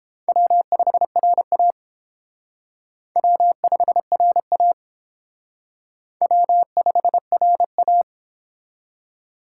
Morse code, heard as a keyed tone, about 31 wpm, 700 Hz. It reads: W5RA W5RA W5RA